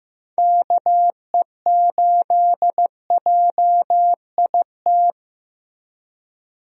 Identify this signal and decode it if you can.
Morse code, heard as a keyed tone, about 15 wpm, 700 Hz. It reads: KE8JIT
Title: USA Call Signs - 6 Characters - Repeated 2x - 15wpm